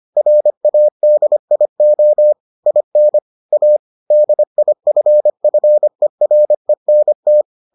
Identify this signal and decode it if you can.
Morse code, heard as a keyed tone, about 25 wpm, 600 Hz. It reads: RADIO IN A DIFFERENT